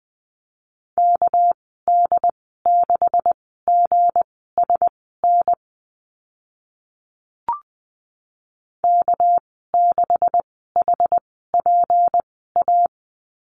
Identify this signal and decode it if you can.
Morse code, heard as a keyed tone, about 20 wpm, spaced as for 14 wpm, 700 Hz. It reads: KD6GSN K6HPA
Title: 213 Lesson 34g - Number 8 - Call Signs 20wpm